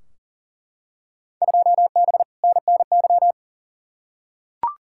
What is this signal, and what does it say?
Morse code, heard as a keyed tone, about 37 wpm, 700 Hz. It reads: 2B NNY
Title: Field Day Exchanges Encore - 40wpm